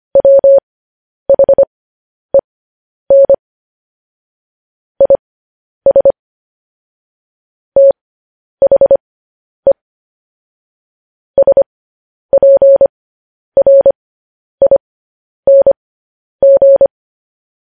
Morse code, heard as a keyed tone, about 25 wpm, spaced as for 10 wpm, 550 Hz. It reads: WHEN IS THE SPRING
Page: news-headlines-in-morse-code-at-10-wpm